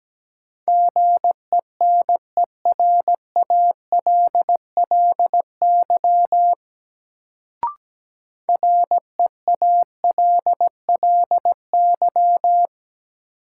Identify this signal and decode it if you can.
Morse code, heard as a keyed tone, about 17 wpm, 700 Hz. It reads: GENERALLY REALLY
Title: Sentences from Top 600 Words - New Word Drill - 17wpm